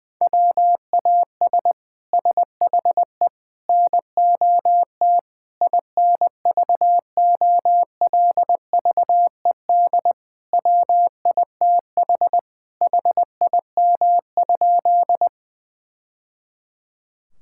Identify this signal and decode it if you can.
Morse code, heard as a keyed tone, about 20 wpm, 700 Hz. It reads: WAS SHE NOT INVOLVED WITH HIM?
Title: Sentences from Top 1000 Words - Review Part 5 - 20wpm